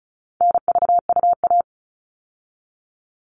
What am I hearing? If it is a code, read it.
Morse code, heard as a keyed tone, about 34 wpm, 700 Hz. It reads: NVUA
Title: All Cut Numbers - 4-Digit 35wpm